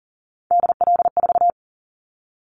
Morse code, heard as a keyed tone, about 40 wpm, 700 Hz. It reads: DL4
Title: Callsign Prefixes for Top 25 DXCC - 40wpm